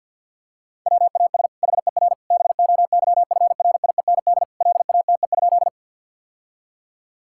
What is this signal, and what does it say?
Morse code, heard as a keyed tone, about 52 wpm, 700 Hz. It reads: WAS HER BOYFRIEND LATE?
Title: Sentences from Top 1000 Words - Review Part 5 - 50wpm